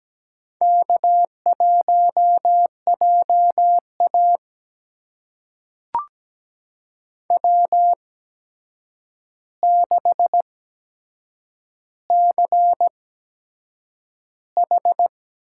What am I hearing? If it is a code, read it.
Morse code, heard as a keyed tone, about 17 wpm, spaced as for 7 wpm, 700 Hz. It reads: K1JA W6CH
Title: USA Call Signs - 4 Characters - 8x Character Spacing - 17wpm